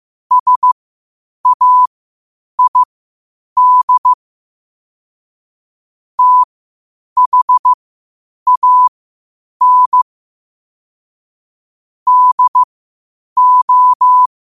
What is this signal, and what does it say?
Morse code, heard as a keyed tone, about 15 wpm, spaced as for 8 wpm, 1000 Hz. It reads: SAID THAN DO
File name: qotdfiles / 8wpm